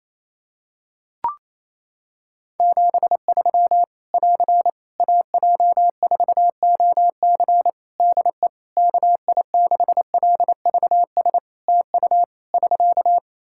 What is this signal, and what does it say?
Morse code, heard as a keyed tone, about 28 wpm, 700 Hz. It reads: 73 <AR> AJ4OC DE KI6LVS TU <SK>
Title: Practice QSO - Exchange 4 - Ending - 28wpm